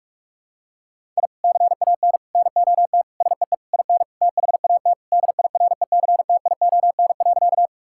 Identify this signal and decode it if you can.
Morse code, heard as a keyed tone, about 41 wpm, 700 Hz. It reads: I CAN NOT SEE IN THAT DIRECTION.
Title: Sentences from Top 1000 Words - Review Part 5 - 45wpm